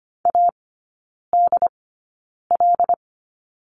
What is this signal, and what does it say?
Morse code, heard as a keyed tone, about 25 wpm, spaced as for 9 wpm, 700 Hz. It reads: ADL
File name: course / 25:10wpm / 093 Lesson 16e - Letter c - Three Characters 25wpm